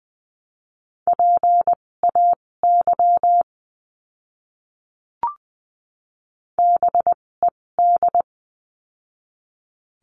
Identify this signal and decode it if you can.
Morse code, heard as a keyed tone, about 20 wpm, spaced as for 16 wpm, 700 Hz. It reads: PAY BED
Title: 256 Lesson 42a - Words 12-20wpm